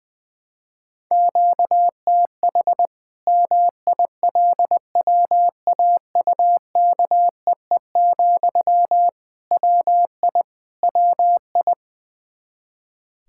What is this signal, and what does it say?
Morse code, heard as a keyed tone, about 20 wpm, 700 Hz. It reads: QTH MILWAUKEE, WI WI
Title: K1USN SST QSO Practice 20wpm